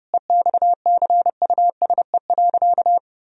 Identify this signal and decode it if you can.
Morse code, heard as a keyed tone, about 32 wpm, 700 Hz. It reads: EXCUSE.
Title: Sentences from Top 1000 Words - Review Part 4 - 30wpm